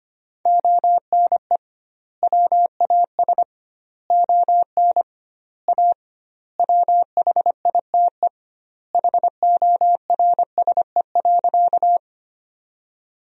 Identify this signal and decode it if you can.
Morse code, heard as a keyed tone, about 25 wpm, 700 Hz. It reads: ONE WAS ON A WHITE HORSE.
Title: 2x Word-Spacing - Sentences from Top 300 Words - 25wpm